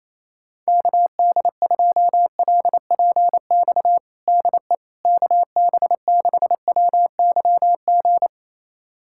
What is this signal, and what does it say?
Morse code, heard as a keyed tone, about 28 wpm, 700 Hz. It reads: KD2LPX DE KB6WYG
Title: USA Callsign DE Callsign - 6 Characters - 28wpm